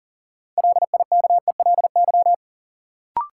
Morse code, heard as a keyed tone, about 40 wpm, 700 Hz. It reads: LIKELY
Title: Sentences from Top 600 Words - New Word Drill - 40wpm